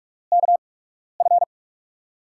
Morse code, heard as a keyed tone, about 44 wpm, spaced as for 12 wpm, 700 Hz. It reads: KF
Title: USA Call Signs - 5 Characters - 8x Character Spacing - 45wpm